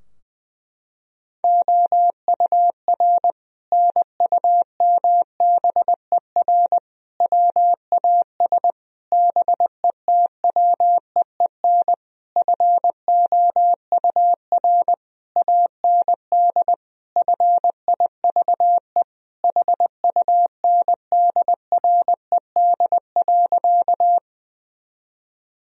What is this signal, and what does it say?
Morse code, heard as a keyed tone, about 20 wpm, 700 Hz. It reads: OUR NUMBER WAS BETWEEN FOUR AND FIVE HUNDRED.
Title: Sentences from Top 1000 Words - Review Part 2 - 20wpm